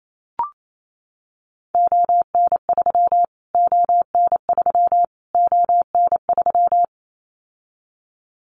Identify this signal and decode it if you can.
Morse code, heard as a keyed tone, about 28 wpm, 700 Hz. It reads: ON3 ON3 ON3
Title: Callsign Prefixes for Top 25 DXCC - Repeated 3x - 28wpm